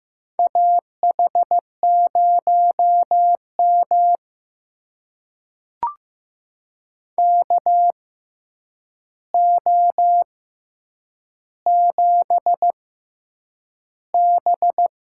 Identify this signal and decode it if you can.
Morse code, heard as a keyed tone, about 15 wpm, spaced as for 8 wpm, 700 Hz. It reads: AH0M KO7B
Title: USA Call Signs - 4 Characters - 6x Character Spacing - 15wpm